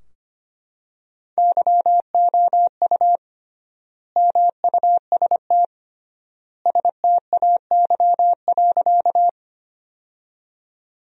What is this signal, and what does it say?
Morse code, heard as a keyed tone, about 25 wpm, 700 Hz. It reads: YOU MUST STAY.
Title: 3x Word-Spacing - Sentences from Top 500 Words - 25wpm